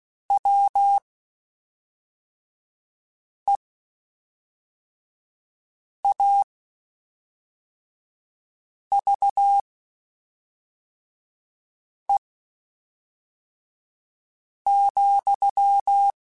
Morse code, heard as a keyed tone, about 16 wpm, spaced as for 3 wpm, 800 Hz. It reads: WEAVE,
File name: Quotes Spaced / Quotes Spaced-mp3s / Deceive sp @5